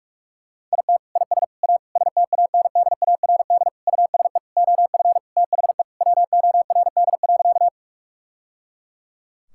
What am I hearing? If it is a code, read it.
Morse code, heard as a keyed tone, about 43 wpm, 700 Hz. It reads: IT IS A STANDARD USE OF THE WORD.
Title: Sentences from Top 1000 Words - Review Part 6 - 45wpm